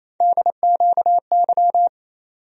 Morse code, heard as a keyed tone, about 28 wpm, 700 Hz. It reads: DQY
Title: USA Call Sign Postfixes - 3 Letters 28wpm